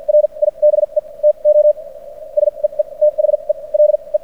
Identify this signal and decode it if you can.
Morse code, heard as a keyed tone, about 57 wpm, spaced as for 37 wpm, 600 Hz. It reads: RIBE TO SEE THELE
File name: archived / CRSnet